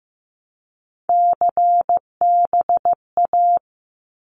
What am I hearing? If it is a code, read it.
Morse code, heard as a keyed tone, about 15 wpm, 700 Hz. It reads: CBA
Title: Abbreviations Repeated 3x - 15wpm